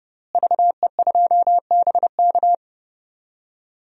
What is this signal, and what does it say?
Morse code, heard as a keyed tone, about 30 wpm, 700 Hz. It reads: VE2BK
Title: DXCC-100 Call Signs - 5 Characters - Repeated 3x - 30wpm